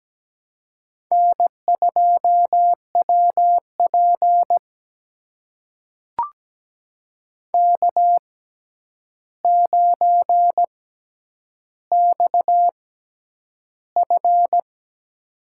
Morse code, heard as a keyed tone, about 17 wpm, spaced as for 9 wpm, 700 Hz. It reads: N2WP K9XF
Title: USA Call Signs - 4 Characters - 6x Character Spacing - 17wpm